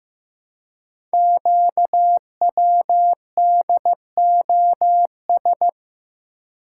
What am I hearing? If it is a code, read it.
Morse code, heard as a keyed tone, about 15 wpm, 700 Hz. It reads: QWDOS
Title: Code Groups - 5 Letters - 15wpm